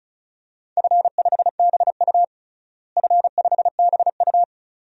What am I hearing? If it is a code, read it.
Morse code, heard as a keyed tone, about 36 wpm, 700 Hz. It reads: F5BU F5BU
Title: DXCC-100 Call Signs - 4 Characters - Repeated 2x - 35wpm